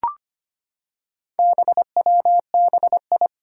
Morse code, heard as a keyed tone, about 25 wpm, 700 Hz. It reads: BWBI
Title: Four Challenging Letter-Number Combinations 25wpm